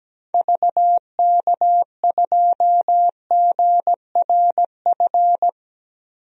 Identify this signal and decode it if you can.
Morse code, heard as a keyed tone, about 17 wpm, 700 Hz. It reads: VK2GRF
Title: DXCC-100 Call Signs - 6 Characters - Repeated 3x - 17wpm